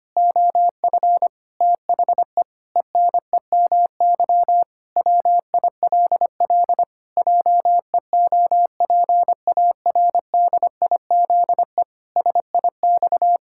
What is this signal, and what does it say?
Morse code, heard as a keyed tone, about 25 wpm, 700 Hz. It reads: OF THE ENEMY WILL JEOPARDIZE SIX
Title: Pangram Challenge v2 - 25wpm